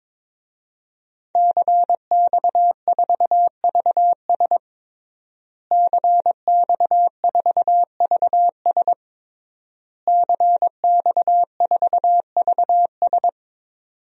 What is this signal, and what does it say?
Morse code, heard as a keyed tone, about 22 wpm, 700 Hz. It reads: CX4VS CX4VS CX4VS
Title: DXCC-100 Call Signs - 5 Characters - Repeated 3x - 22wpm